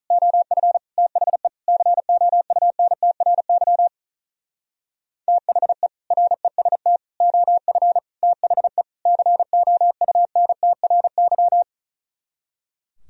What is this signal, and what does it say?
Morse code, heard as a keyed tone, about 44 wpm, 700 Hz. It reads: OF THE COUNTRY THE REST OF THE COUNTRY
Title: Speed-Racing - Sets of 5 Words 35wpm